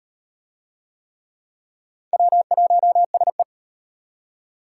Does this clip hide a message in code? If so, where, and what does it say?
Yes: Morse code from 2.1 to 3.4 s, heard as a keyed tone, about 34 wpm, 700 Hz. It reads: W1SE